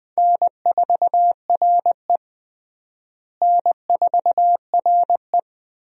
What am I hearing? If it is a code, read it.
Morse code, heard as a keyed tone, about 20 wpm, 700 Hz. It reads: N4RE N4RE
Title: USA Call Signs - 4 Characters - Repeated 3x - 20wpm